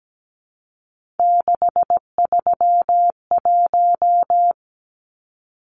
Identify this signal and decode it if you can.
Morse code, heard as a keyed tone, about 17 wpm, 700 Hz. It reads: 631
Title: Numbers - 3-Digit 17wpm